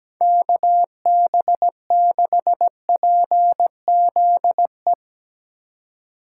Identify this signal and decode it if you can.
Morse code, heard as a keyed tone, about 18 wpm, 700 Hz. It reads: KB6PZE
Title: Speed Racing - USA 2x3 Call Signs 17wpm